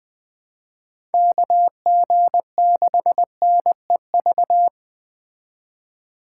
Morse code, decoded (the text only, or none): KG6NEV